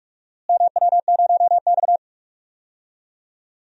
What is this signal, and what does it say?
Morse code, heard as a keyed tone, about 42 wpm, 700 Hz. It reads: MW0X
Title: DXCC-100 Call Signs - 4 Characters - Repeated 2x - 45wpm